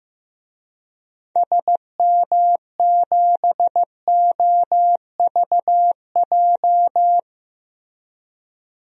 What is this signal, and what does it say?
Morse code, heard as a keyed tone, about 15 wpm, 700 Hz. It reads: SM7OVJ
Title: DXCC-100 Call Signs - 6 Characters - 15wpm